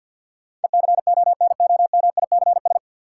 Morse code, heard as a keyed tone, about 53 wpm, 700 Hz. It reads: ECONOMICS
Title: Sentences from Top 1000 Words - Review Part 3 - New Word Drill - 50wpm